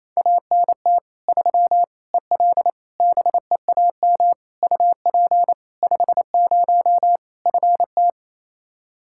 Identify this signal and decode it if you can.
Morse code, heard as a keyed tone, about 28 wpm, 700 Hz. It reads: ANT 3 EL BEAM UP 50 FT
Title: Practice QSO - Exchange 2 - Info - 28wpm